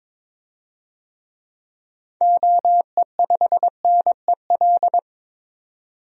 Morse code, heard as a keyed tone, about 21 wpm, 700 Hz. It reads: OE5NEL
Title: DXCC-100 Call Signs - 6 Characters - 22wpm